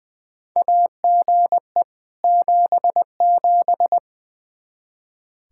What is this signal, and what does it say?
Morse code, heard as a keyed tone, about 20 wpm, 700 Hz. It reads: AGE 77
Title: Practice QSO - Exchange 3 - More Info - 20wpm